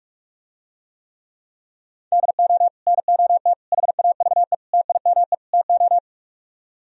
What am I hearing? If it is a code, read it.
Morse code, heard as a keyed tone, about 47 wpm, 700 Hz. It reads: DO NOT HAVE TIME TO